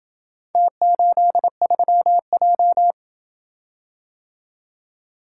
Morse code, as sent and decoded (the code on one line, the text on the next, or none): - ---.. ...-- .---
T83J